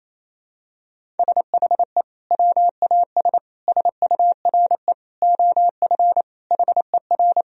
Morse code, heard as a keyed tone, about 28 wpm, 700 Hz. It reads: SHE WAS SURE OF HER
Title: Sentences from Top 1000 Words - Review Part 1 - 28wpm